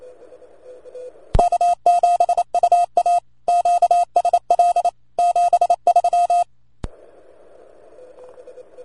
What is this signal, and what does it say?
Morse code, heard as a keyed tone, about 28 wpm, 700 Hz. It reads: K7UA QSL 73